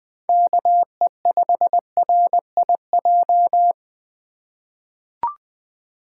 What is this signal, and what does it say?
Morse code, heard as a keyed tone, about 20 wpm, 700 Hz. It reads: KE5RIJ
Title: USA Call Signs - 4x Character Spacing - 20wpm